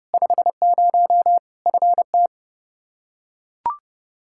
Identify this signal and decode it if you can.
Morse code, heard as a keyed tone, about 30 wpm, 700 Hz. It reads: 50 FT
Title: Practice QSO - Exchange 2 - Info - 30wpm